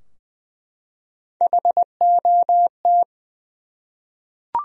Morse code, heard as a keyed tone, about 20 wpm, 700 Hz. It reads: HOT
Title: Top 100 Words Repeated 20wpm